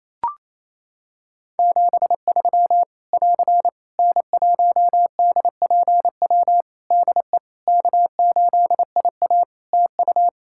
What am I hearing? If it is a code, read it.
Morse code, heard as a keyed tone, about 28 wpm, 700 Hz. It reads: 73 <AR> N1DPW DE K8IA TU
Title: K1USN SST QSO Practice 28wpm